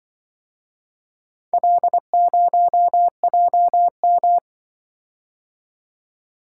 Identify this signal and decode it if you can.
Morse code, heard as a keyed tone, about 24 wpm, 700 Hz. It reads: L0JM
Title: Speed-Racing - Four Letter-Number Combinations - 20wpm